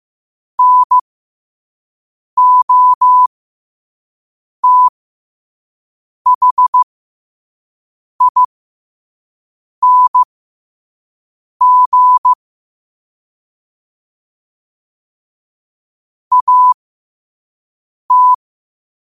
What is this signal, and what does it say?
Morse code, heard as a keyed tone, about 15 wpm, spaced as for 5 wpm, 1000 Hz. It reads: NOTHING AT
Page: New quotes every day in morse code at 5 Words per minute.